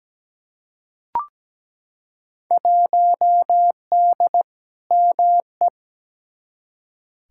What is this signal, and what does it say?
Morse code, heard as a keyed tone, about 17 wpm, 700 Hz. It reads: 1D ME
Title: Field Day Exchanges - 17wpm